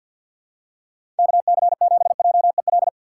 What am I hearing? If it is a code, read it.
Morse code, heard as a keyed tone, about 53 wpm, 700 Hz. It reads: KC7JEL
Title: USA Call Signs - 6 Characters - 3x Character Spacing - 50wpm